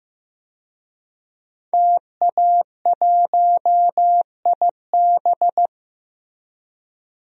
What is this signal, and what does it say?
Morse code, heard as a keyed tone, about 15 wpm, 700 Hz. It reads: TA1IB
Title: DXCC-100 Call Signs - 5 Characters - 15wpm